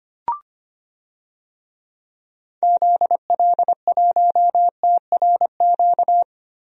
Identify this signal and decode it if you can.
Morse code, heard as a keyed tone, about 25 wpm, 700 Hz. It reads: ZL1TRQ
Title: DXCC-100 Call Signs - 6 Characters - Repeated 2x - 25wpm